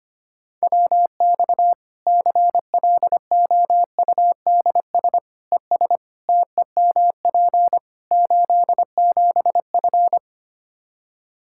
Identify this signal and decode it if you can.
Morse code, heard as a keyed tone, about 25 wpm, 700 Hz. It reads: WX CLOUDS ES TEMP 87F
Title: Practice QSO - Exchange 2 - Info - 25wpm